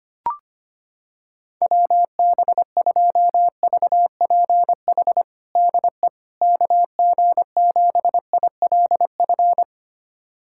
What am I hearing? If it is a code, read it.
Morse code, heard as a keyed tone, about 25 wpm, 700 Hz. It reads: WB2VPH DE KG7ILF
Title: USA Callsign DE Callsign 25wpm